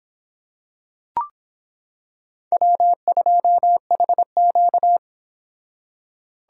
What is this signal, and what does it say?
Morse code, heard as a keyed tone, about 25 wpm, 700 Hz. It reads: W2HQ